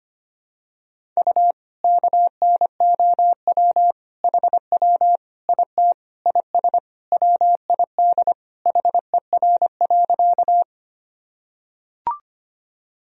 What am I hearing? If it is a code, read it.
Morse code, heard as a keyed tone, about 25 wpm, 700 Hz. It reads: U KNOW HW IT IS WID HER.
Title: Sentences from Top 100 Words and Abbreviations 25wpm